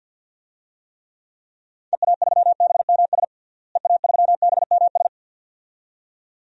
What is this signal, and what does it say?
Morse code, heard as a keyed tone, about 52 wpm, 700 Hz. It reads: EA3BMS EA3BMS
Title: DXCC-100 Call Signs - 6 Characters - Repeated 2x - 50wpm